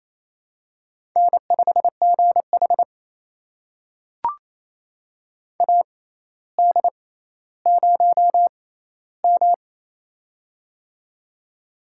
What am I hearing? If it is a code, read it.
Morse code, heard as a keyed tone, about 27 wpm, spaced as for 14 wpm, 700 Hz. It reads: N5GH AD0M
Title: USA Call Signs - 4 Characters - 6x Character Spacing - 28wpm